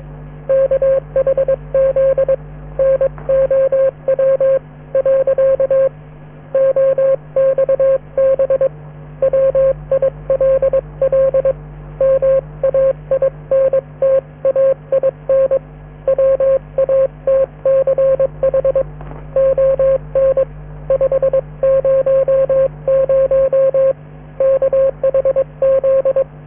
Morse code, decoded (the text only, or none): KHZ NOW . OXB WILL MAINTAIN WATCH ON 500 KHZ